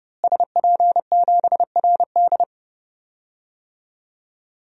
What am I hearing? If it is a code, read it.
Morse code, heard as a keyed tone, about 30 wpm, 700 Hz. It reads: SP7RD
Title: DXCC-100 Call Signs - 5 Characters - Repeated 2x - 30wpm